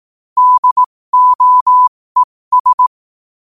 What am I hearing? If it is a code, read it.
Morse code, heard as a keyed tone, about 18 wpm, spaced as for 15 wpm, 1000 Hz. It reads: DOES
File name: qotdfiles / 15wpm